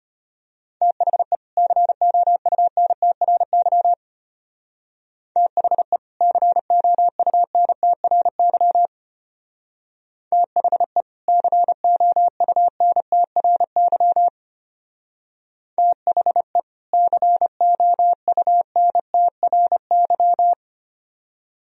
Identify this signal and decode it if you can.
Morse code, heard as a keyed tone, about 29 wpm, 700 Hz. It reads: THE COUNTRY THE COUNTRY THE COUNTRY THE COUNTRY
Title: Speed-Racing - Sets of 2 Words 25wpm